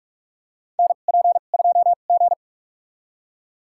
Morse code, heard as a keyed tone, about 46 wpm, spaced as for 33 wpm, 700 Hz. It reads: NP2G
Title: USA Call Signs - 4 Characters - 2x Character Spacing - 45wpm